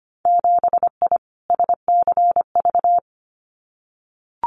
Morse code, heard as a keyed tone, about 25 wpm, 700 Hz. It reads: 7I SCV
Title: Winter Field Day Exchanges Encore 25wpm